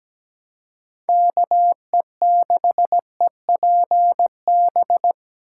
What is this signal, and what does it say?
Morse code, heard as a keyed tone, about 20 wpm, 700 Hz. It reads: KE6EPB
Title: Speed Racing - USA 2x3 Call Signs 17wpm